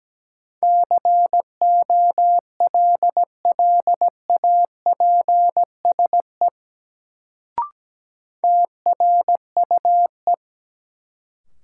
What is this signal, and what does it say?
Morse code, heard as a keyed tone, about 17 wpm, 700 Hz. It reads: COLLAPSE TRUE
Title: Sets of 5 Words Encore - New Word Drill - 17wpm